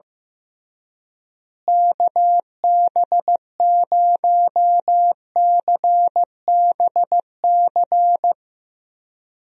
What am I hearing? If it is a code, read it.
Morse code, heard as a keyed tone, about 15 wpm, 700 Hz. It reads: KB0CBC
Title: USA Call Signs - 6 Characters - Repeated 2x - 15wpm